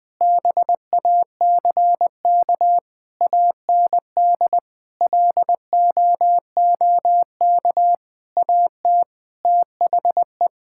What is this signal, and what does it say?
Morse code, heard as a keyed tone, about 20 wpm, 700 Hz. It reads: BACK AND LOOK AT THE
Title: Sentences with 8-Word Phrases - 20wpm